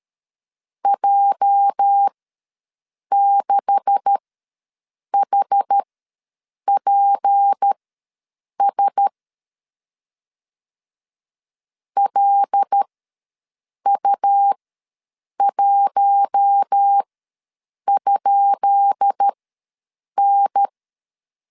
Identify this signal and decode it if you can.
Morse code, heard as a keyed tone, about 13 wpm, spaced as for 7 wpm, 800 Hz. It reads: J6HPS LU1?N
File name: LEZIONI / 10 SIGNS / 10B INT ws 12wpm 3527